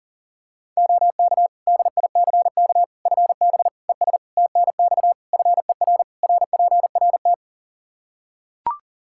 Morse code, heard as a keyed tone, about 40 wpm, 700 Hz. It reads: OK DICK FB ES TNX FER RPRT
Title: Practice QSO - Exchange 2 - Info - 40wpm